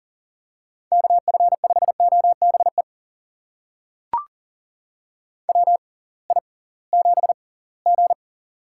Morse code, heard as a keyed tone, about 40 wpm, spaced as for 23 wpm, 700 Hz. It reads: KF5OBE WI7G
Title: USA Call Signs - 6x Character Spacing - 40wpm